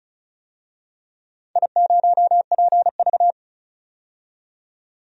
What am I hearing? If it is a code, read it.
Morse code, heard as a keyed tone, about 36 wpm, 700 Hz. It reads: I0PV
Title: DXCC-100 Call Signs - 4 Characters - 35wpm